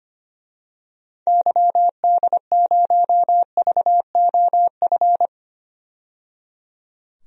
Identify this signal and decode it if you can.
Morse code, heard as a keyed tone, about 25 wpm, 700 Hz. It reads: YD0VOF
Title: DXCC-100 Call Signs - 6 Characters - 25wpm